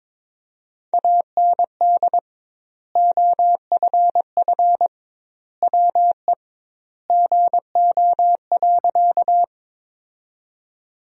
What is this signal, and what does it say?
Morse code, heard as a keyed tone, about 22 wpm, 700 Hz. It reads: AND OFF WE GO.
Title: 2x Word-Spacing - Sentences from Top 200 Words - 22wpm